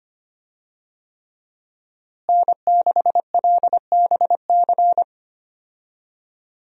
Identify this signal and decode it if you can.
Morse code, heard as a keyed tone, about 25 wpm, 700 Hz. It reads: N6LBC
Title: USA Call Signs - 5 Characters - Repeated 3x - 25wpm